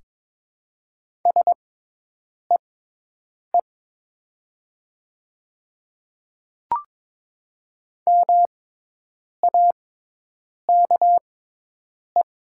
Morse code, heard as a keyed tone, about 22 wpm, spaced as for 8 wpm, 700 Hz. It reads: SEE MAKE
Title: 6x Character Spacing - Top 500 Words - 22wpm